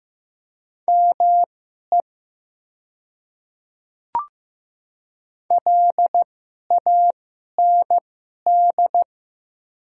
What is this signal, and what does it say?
Morse code, heard as a keyed tone, about 15 wpm, spaced as for 11 wpm, 700 Hz. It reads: ME LAND
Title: 2x Character Spacing - Top 300 Words - 15wpm